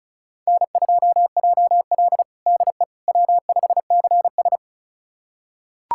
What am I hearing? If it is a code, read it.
Morse code, heard as a keyed tone, about 36 wpm, 700 Hz. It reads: N2JL DE W5CS
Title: USA Callsign DE Callsign - 4 Characters - 35wpm